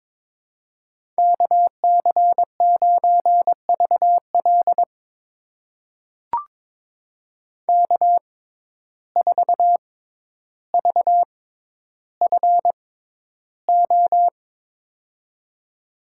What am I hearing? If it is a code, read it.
Morse code, heard as a keyed tone, about 22 wpm, spaced as for 11 wpm, 700 Hz. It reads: KC9VL K4VFO
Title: USA Call Signs - 5 Characters - 6x Character Spacing - 22wpm